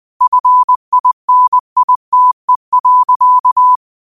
Morse code, heard as a keyed tone, about 20 wpm, 1000 Hz. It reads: FINITE.
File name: qotdfiles / 20wpm